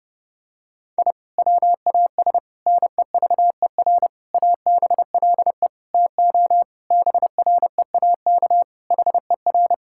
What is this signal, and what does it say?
Morse code, heard as a keyed tone, about 30 wpm, 700 Hz. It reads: I WAS NEVER ABLE TO BREAK HER
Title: Sentences from Top 1000 Words - Review Part 6 - 30wpm